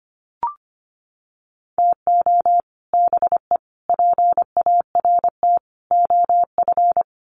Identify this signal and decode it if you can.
Morse code, heard as a keyed tone, about 25 wpm, 700 Hz. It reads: TO BE PART OF
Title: Sets of 4 Words 25wpm